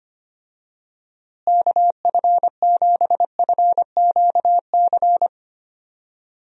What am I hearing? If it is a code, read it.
Morse code, heard as a keyed tone, about 25 wpm, 700 Hz. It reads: KF7FQC
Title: USA Call Signs - 6 Characters - Repeated 2x - 25wpm